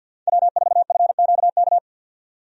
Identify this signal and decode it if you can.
Morse code, heard as a keyed tone, about 50 wpm, 700 Hz. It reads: W4FQK
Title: USA Call Signs - 5 Characters - 50wpm